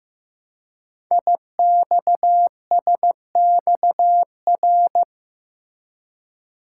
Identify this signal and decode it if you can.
Morse code, heard as a keyed tone, about 15 wpm, 700 Hz. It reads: IXSXR
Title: Code Groups - 5 Letter:Numbers - 15wpm